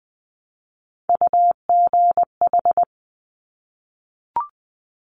Morse code, heard as a keyed tone, about 20 wpm, 700 Hz. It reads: UGH
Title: Top Three Letter Combinations in Words 20wpm